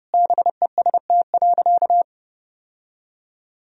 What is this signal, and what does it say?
Morse code, heard as a keyed tone, about 30 wpm, 700 Hz. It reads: BEST.
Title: Sentences from Top 1000 Words - Review Part 2 - 30wpm